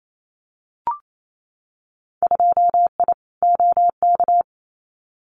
Morse code, heard as a keyed tone, about 28 wpm, 700 Hz. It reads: 2I OK
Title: Winter Field Day Exchanges Encore 28wpm